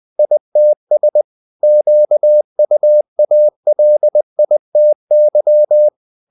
Morse code, heard as a keyed tone, about 20 wpm, 600 Hz. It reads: ITS QUALITY
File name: the old man and the sea part 2 20wpm